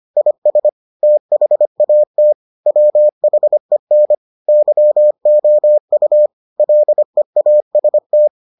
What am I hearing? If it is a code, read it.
Morse code, heard as a keyed tone, about 25 wpm, 600 Hz. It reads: IS THAT WHEN YOU LEAST